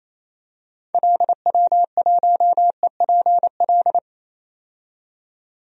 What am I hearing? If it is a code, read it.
Morse code, heard as a keyed tone, about 28 wpm, 700 Hz. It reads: LW1EPL
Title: DXCC-100 Call Signs - 6 Characters - 28wpm